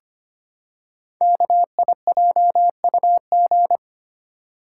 Morse code, heard as a keyed tone, about 25 wpm, 700 Hz. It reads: KIJUG
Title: Code Groups - 5 Letters - 25wpm